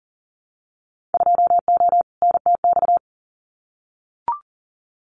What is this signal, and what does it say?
Morse code, heard as a keyed tone, about 38 wpm, 700 Hz. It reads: 2O NTX
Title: Winter Field Day Exchanges 40wpm